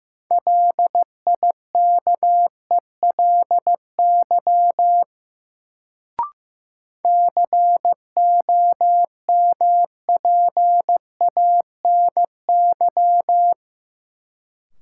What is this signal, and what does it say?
Morse code, heard as a keyed tone, about 15 wpm, 700 Hz. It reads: LIKELY COMPANY
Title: Sentences from Top 600 Words - New Word Drill - 15wpm